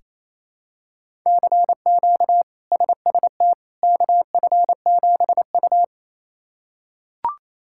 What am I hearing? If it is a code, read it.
Morse code, heard as a keyed tone, about 28 wpm, 700 Hz. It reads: CQ SST KF7U
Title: K1USN SST QSO Practice - 28wpm